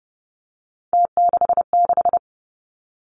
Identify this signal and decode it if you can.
Morse code, heard as a keyed tone, about 30 wpm, 700 Hz. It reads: T66
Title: Three Challenging Letter-Number Combinations 30wpm